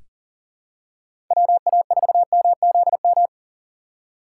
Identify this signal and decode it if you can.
Morse code, heard as a keyed tone, about 40 wpm, 700 Hz. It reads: WA4MZM
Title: USA Call Signs - 6 Characters - 6x Character Spacing - 40wpm